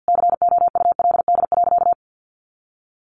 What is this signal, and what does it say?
Morse code, heard as a keyed tone, about 50 wpm, 700 Hz. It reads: COULD.
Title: Sentences from Top 500 Words 50wpm